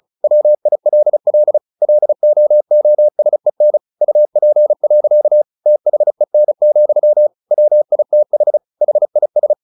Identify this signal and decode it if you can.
Morse code, heard as a keyed tone, about 34 wpm, 600 Hz. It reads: WILL LOOSEN UP. THEN, WITH HIS
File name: the old man and the sea / the old man and the sea 1 35wpm